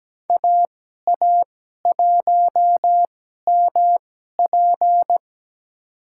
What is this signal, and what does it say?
Morse code, heard as a keyed tone, about 17 wpm, spaced as for 12 wpm, 700 Hz. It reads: AA1MP
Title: USA Call Signs - 5 Characters - 2x Character Spacing - 17wpm